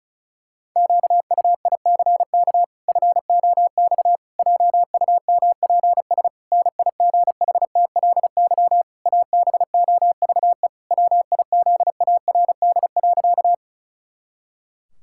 Morse code, heard as a keyed tone, about 36 wpm, 700 Hz. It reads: QUICK FOX JUMPS NIGHTLY ABOVE WIZARD.
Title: Pangram Challenge v2 - 35wpm